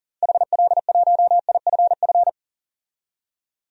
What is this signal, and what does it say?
Morse code, heard as a keyed tone, about 37 wpm, 700 Hz. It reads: HL1IFF
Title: DXCC-100 Call Signs - 6 Characters - 40wpm